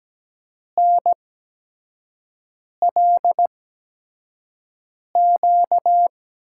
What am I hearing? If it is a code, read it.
Morse code, heard as a keyed tone, about 17 wpm, spaced as for 5 wpm, 700 Hz. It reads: NLQ
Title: USA Call Signs - 8x Character Spacing - 17wpm